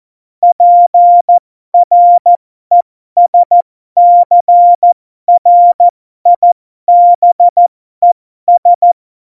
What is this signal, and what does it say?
Morse code, heard as a keyed tone, about 14 wpm, spaced as for 12 wpm, 700 Hz. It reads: PRESCRIBES
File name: Morsekurs / Witze